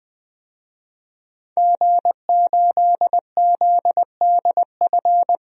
Morse code, heard as a keyed tone, about 20 wpm, 700 Hz. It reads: G8ZDF
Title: DXCC-100 Call Signs - 5 Characters - 20wpm